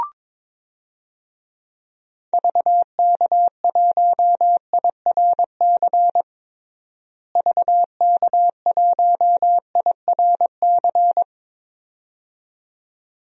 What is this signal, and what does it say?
Morse code, heard as a keyed tone, about 22 wpm, 700 Hz. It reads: VK1IRC VK1IRC
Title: DXCC-100 Call Signs - 6 Characters - Repeated 2x - 22wpm